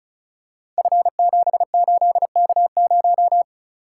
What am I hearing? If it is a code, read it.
Morse code, heard as a keyed tone, about 36 wpm, 690 Hz. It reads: F78K0
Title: Code Groups - 5 Letter:Numbers - 35wpm